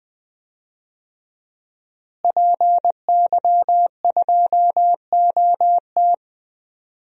Morse code, heard as a keyed tone, about 20 wpm, 700 Hz. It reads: PY2OT
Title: DXCC-100 Call Signs - 5 Characters - Repeated 3x - 20wpm